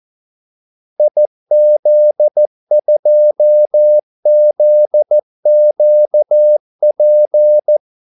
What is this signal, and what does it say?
Morse code, heard as a keyed tone, about 14 wpm, 600 Hz. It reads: IZ2ZQP
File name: Callsigns-14wpm